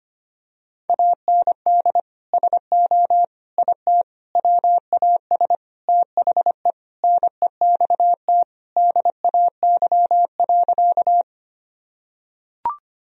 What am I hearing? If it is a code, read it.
Morse code, heard as a keyed tone, about 25 wpm, 700 Hz. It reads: AND SO IT WAS THE NEXT DAY.
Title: Sentences from Top 1000 Words - Review Part 1 - 25wpm